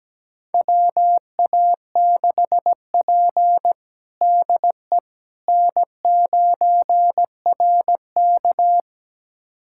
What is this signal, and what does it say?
Morse code, heard as a keyed tone, about 17 wpm, 700 Hz. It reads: WA6P DE N9RK
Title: USA Callsign DE Callsign - 4 Characters - 17wpm